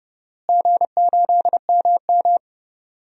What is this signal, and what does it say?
Morse code, heard as a keyed tone, about 30 wpm, 700 Hz. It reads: G8MM
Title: DXCC-100 Call Signs - 4 Characters - Repeated 2x - 30wpm